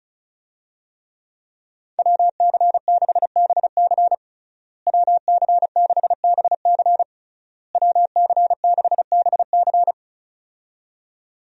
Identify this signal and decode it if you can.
Morse code, heard as a keyed tone, about 36 wpm, 700 Hz. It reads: WC6BC WC6BC WC6BC
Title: USA Call Signs - 5 Characters - Repeated 3x - 35wpm